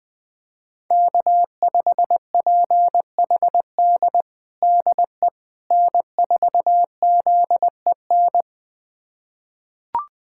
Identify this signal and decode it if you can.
Morse code, heard as a keyed tone, about 21 wpm, 700 Hz. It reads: K5PHD DE N4ZEN
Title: USA Callsign DE Callsign - 5 Characters - 20wpm